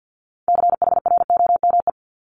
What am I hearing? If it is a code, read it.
Morse code, heard as a keyed tone, about 50 wpm, 700 Hz. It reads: CHROME
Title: Multi-letter Phonemes 50wpm